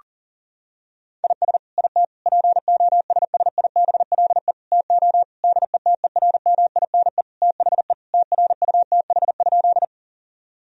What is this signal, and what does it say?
Morse code, heard as a keyed tone, about 40 wpm, 700 Hz. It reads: IS IT POSSIBLE TO DETERMINE THE TRUTH?
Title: Sentences from Top 1000 Words - Review Part 4 - 40wpm